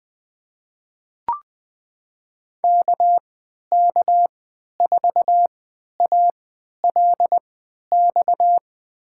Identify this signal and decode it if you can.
Morse code, heard as a keyed tone, about 21 wpm, spaced as for 12 wpm, 700 Hz. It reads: KK4ALX
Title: USA Call Signs - 6 Characters - 3x Character Spacing - 20wpm